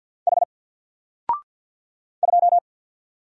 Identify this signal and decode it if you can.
Morse code, heard as a keyed tone, about 50 wpm, 700 Hz. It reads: H 2
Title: Single Character (Letter:Number:Punctuation:Prosign) Alt 50wpm